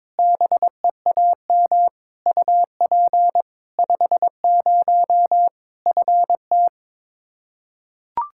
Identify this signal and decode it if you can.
Morse code, heard as a keyed tone, about 22 wpm, 700 Hz. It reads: BEAM UP 50 FT
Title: Practice QSO - Exchange 2 - Info - 22wpm